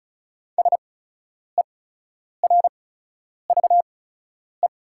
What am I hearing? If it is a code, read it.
Morse code, heard as a keyed tone, about 36 wpm, spaced as for 10 wpm, 700 Hz. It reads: SERVE
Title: 8x Character Spacing - Top 500 Words - 35wpm